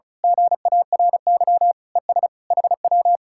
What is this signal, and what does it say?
Morse code, heard as a keyed tone, about 33 wpm, 700 Hz. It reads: GARY ES HW
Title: Speed-Racing Common QSO Phrases - 35wpm